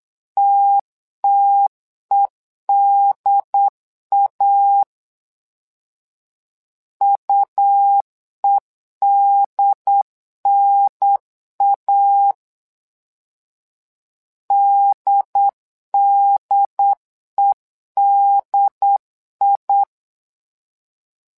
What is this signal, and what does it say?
Morse code, heard as a keyed tone, about 8 wpm, 800 Hz. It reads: TTEDA UEDNA DDEDI
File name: LEZIONI / 2 UHDN / 2 UHDN 8wpm2232